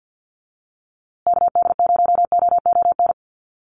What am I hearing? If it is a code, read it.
Morse code, heard as a keyed tone, about 50 wpm, 700 Hz. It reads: KD0OON
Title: USA Call Signs Repeated 2x 50wpm